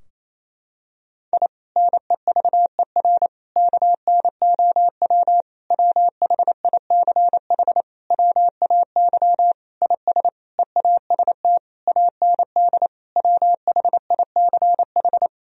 Morse code, heard as a keyed tone, about 28 wpm, 700 Hz. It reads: I NEVER KNOW WHICH WAY IS EAST AND WHICH
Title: Sentences from Top 1000 Words - Review Part 3 - 28wpm